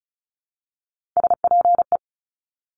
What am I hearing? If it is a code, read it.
Morse code, heard as a keyed tone, about 36 wpm, 700 Hz. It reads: SPE
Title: Top Three Letter Combinations in Words 35wpm